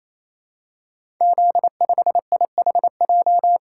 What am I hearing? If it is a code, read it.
Morse code, heard as a keyed tone, about 27 wpm, 700 Hz. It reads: Z5IHJ
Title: Code Groups - 5 Letter:Numbers - 28wpm